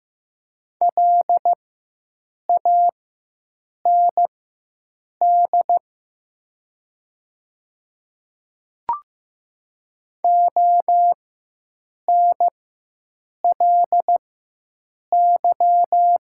Morse code, heard as a keyed tone, about 15 wpm, spaced as for 7 wpm, 700 Hz. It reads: LAND ONLY
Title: 4x Character Spacing - Top 300 Words - 15wpm